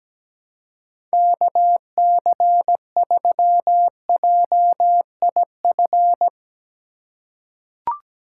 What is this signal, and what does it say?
Morse code, heard as a keyed tone, about 17 wpm, 700 Hz. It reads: KC3JIF
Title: USA Call Signs - 6 Characters - 3x Character Spacing - 17wpm